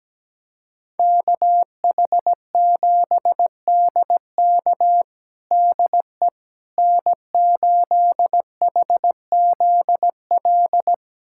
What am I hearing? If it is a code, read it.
Morse code, heard as a keyed tone, about 17 wpm, 700 Hz. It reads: KH7DK DE N8HZL
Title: USA Callsign DE Callsign - 5 Characters - 17wpm